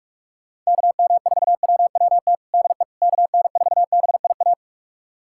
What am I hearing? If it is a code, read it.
Morse code, heard as a keyed tone, about 42 wpm, 690 Hz. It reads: KM4WWT DE KN4BIA
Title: USA Callsign DE Callsign 45wpm